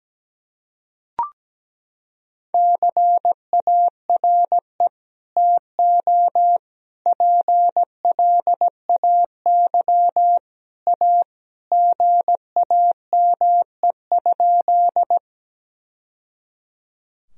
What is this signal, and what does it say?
Morse code, heard as a keyed tone, about 17 wpm, 700 Hz. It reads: CARE TO PLAY A GAME?
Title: Sentences from Top 1000 Words - Review Part 1 - 17wpm